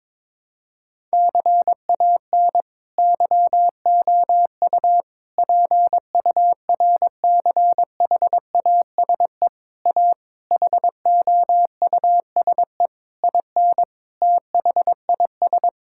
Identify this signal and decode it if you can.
Morse code, heard as a keyed tone, about 22 wpm, 700 Hz. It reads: CAN YOU PURCHASE A HOUSE IN THIS
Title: Sentences from Top 1000 Words - Review Part 5 - 22wpm